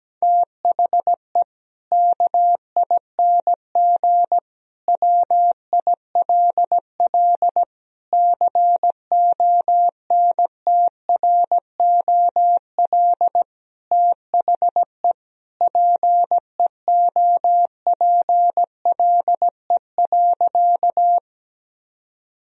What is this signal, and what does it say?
Morse code, heard as a keyed tone, about 17 wpm, 700 Hz. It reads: THE KING WILL CONTROL THE PEOPLE.
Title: Sentences from Top 1000 Words - Review Part 3 - 17wpm